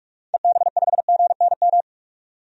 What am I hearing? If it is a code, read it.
Morse code, heard as a keyed tone, about 46 wpm, 700 Hz. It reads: EB5GNM
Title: DXCC-100 Call Signs - 6 Characters - 45wpm